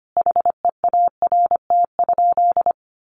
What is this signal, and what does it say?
Morse code, heard as a keyed tone, about 25 wpm, 700 Hz. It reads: HEART?
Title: Sentences from Top 1200 Words 25wpm